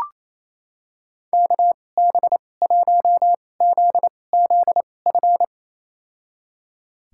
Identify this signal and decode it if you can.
Morse code, heard as a keyed tone, about 28 wpm, spaced as for 20 wpm, 700 Hz. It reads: KB1ZZF
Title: USA Call Signs - 6 Characters - 2x Character Spacing - 28wpm